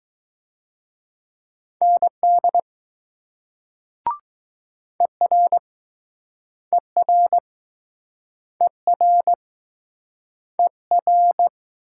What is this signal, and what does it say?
Morse code, heard as a keyed tone, about 20 wpm, 700 Hz. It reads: ND ER ER ER ER